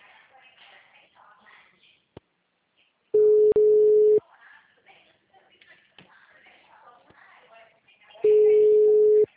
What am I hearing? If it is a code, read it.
Ringback tone (Europe, 425 Hz), with speech.
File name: out- -2021- - - .916